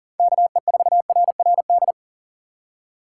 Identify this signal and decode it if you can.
Morse code, heard as a keyed tone, about 48 wpm, 700 Hz. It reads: KE4RRD
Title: Speed Racing - USA 2x3 Call Signs 40wpm